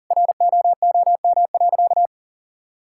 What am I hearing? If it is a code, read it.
Morse code, heard as a keyed tone, about 40 wpm, 700 Hz. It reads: ROOM.
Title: Sentences from Top 1000 Words - Review Part 5 - 40wpm